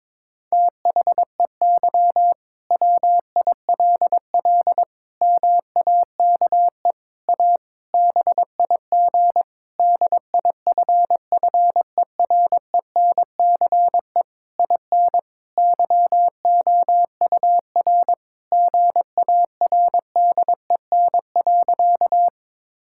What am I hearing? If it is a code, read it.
Morse code, heard as a keyed tone, about 22 wpm, 700 Hz. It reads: THEY WILL MAKE A BIG DIFFERENCE IN YOUR GARDEN.
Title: Sentences from Top 1000 Words - Review Part 5 - 22wpm